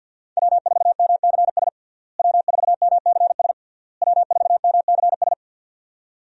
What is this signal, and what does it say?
Morse code, heard as a keyed tone, about 52 wpm, 700 Hz. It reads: W4MCS W4MCS W4MCS
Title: USA Call Signs - 5 Characters - Repeated 3x - 50wpm